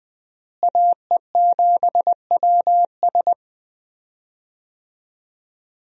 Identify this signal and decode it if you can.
Morse code, heard as a keyed tone, about 20 wpm, 700 Hz. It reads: AE7WS
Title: USA Call Signs - 5 Characters - Repeated 2x - 20wpm